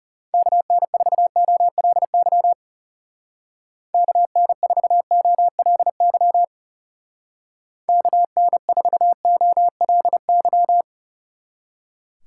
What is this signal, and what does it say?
Morse code, heard as a keyed tone, about 36 wpm, 700 Hz. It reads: KN4OLY KN4OLY KN4OLY
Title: Speed Racing - USA 2x3 Call Signs 30wpm